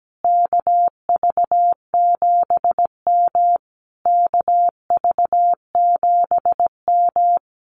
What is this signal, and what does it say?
Morse code, heard as a keyed tone, about 17 wpm, 700 Hz. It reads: KV7M KV7M
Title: Common QSO Phrases Encore 17wpm